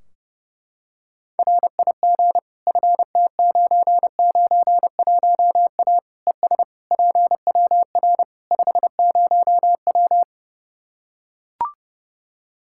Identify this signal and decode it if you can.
Morse code, heard as a keyed tone, about 32 wpm, 700 Hz. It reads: RIG FT991A ES PWR 50W
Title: K1USN SST QSO Practice 30wpm